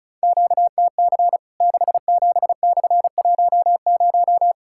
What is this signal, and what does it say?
Morse code, heard as a keyed tone, about 34 wpm, 700 Hz. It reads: QTC 67/10
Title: Worked All Europe DX Contest - QTC Exchange - 35wpm